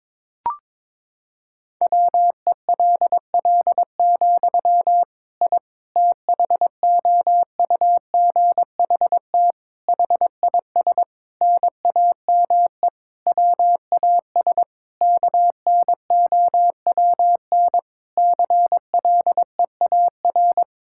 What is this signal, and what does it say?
Morse code, heard as a keyed tone, about 22 wpm, 700 Hz. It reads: WELL, I THOUGHT HIS NAME WAS KNOWN CLEAR
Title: Sentences from Top 1000 Words - Review Part 5 - 22wpm